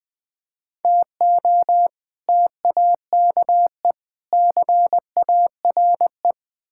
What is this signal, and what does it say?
Morse code, heard as a keyed tone, about 21 wpm, 700 Hz. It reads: TO TAKE CARE
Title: Build A Phrase - 4-Word Phrases - 20wpm